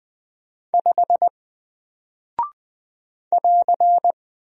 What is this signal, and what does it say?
Morse code, heard as a keyed tone, about 20 wpm, 700 Hz. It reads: 5 <AR>
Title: Single Character (Letter:Number:Punctuation:Prosign) 20wpm